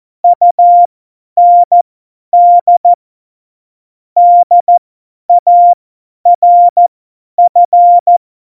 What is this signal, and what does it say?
Morse code, heard as a keyed tone, about 14 wpm, spaced as for 10 wpm, 700 Hz. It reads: UND DARF
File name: Morsekurs / Witze / Witz 28 April 2020, 50 F70